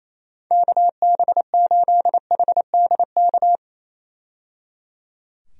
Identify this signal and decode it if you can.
Morse code, heard as a keyed tone, about 28 wpm, 700 Hz. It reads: KB8HDK
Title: USA Call Signs - 6 Characters - 28wpm